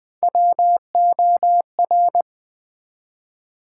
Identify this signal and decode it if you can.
Morse code, heard as a keyed tone, about 20 wpm, 700 Hz. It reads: WOR
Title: Build A Word - 1 Letter At A Time - 3 to 5-Letter Words - 20wpm